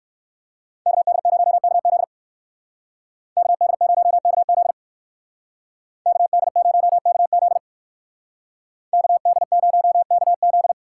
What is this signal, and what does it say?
Morse code, heard as a keyed tone, about 58 wpm, 700 Hz. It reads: KD0KZ KD0KZ KD0KZ KD0KZ